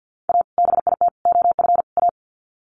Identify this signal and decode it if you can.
Morse code, heard as a keyed tone, about 50 wpm, 700 Hz. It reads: A BIT OF A
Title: Sets of 4 Words 50wpm